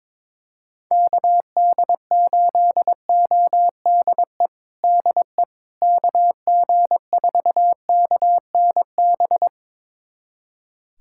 Morse code, heard as a keyed tone, about 21 wpm, 700 Hz. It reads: KD8ODE DE KG4KNB
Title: USA Callsign DE Callsign - 6 Characters - 22wpm